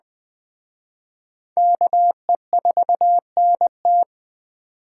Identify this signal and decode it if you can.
Morse code, heard as a keyed tone, about 20 wpm, 700 Hz. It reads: KE4NT